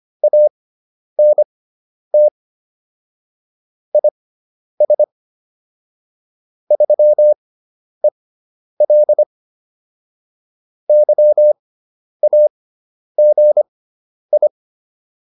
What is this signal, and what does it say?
Morse code, heard as a keyed tone, about 25 wpm, spaced as for 10 wpm, 600 Hz. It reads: ANT IS 3EL YAGI